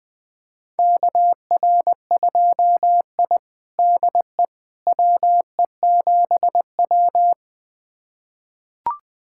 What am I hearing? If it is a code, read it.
Morse code, heard as a keyed tone, about 20 wpm, 700 Hz. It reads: KR2I DE WE7W
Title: USA Callsign DE Callsign - 4 Characters - 20wpm